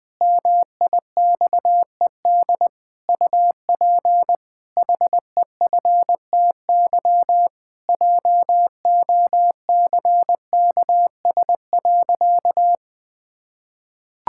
Morse code, heard as a keyed tone, about 21 wpm, 700 Hz. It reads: MIXED UP HEFTY JOCKS.
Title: Pangram Challenge v2 - 20wpm